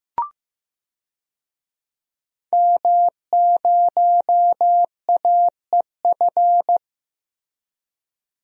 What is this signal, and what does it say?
Morse code, heard as a keyed tone, about 15 wpm, 700 Hz. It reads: M0AEF
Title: DXCC-100 Call Signs - 5 Characters - Repeated 2x - 15wpm